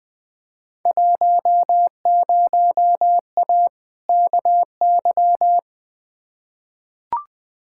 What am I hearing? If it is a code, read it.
Morse code, heard as a keyed tone, about 20 wpm, 700 Hz. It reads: 10A KY
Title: Field Day Exchanges Encore - 20wpm